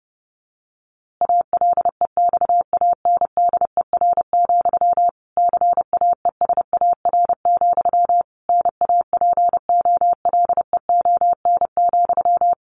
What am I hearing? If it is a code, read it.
Morse code, heard as a keyed tone, about 30 wpm, 700 Hz. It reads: ALEXANDER, CAESAR, NAPOLEON,
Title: A Sound Of Thunder 30wpm